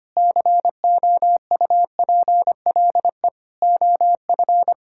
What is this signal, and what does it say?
Morse code, heard as a keyed tone, about 25 wpm, 700 Hz. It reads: COUPLE OF
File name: practice / 25wpm / Sentences with 4-Word Phrases - 25wpm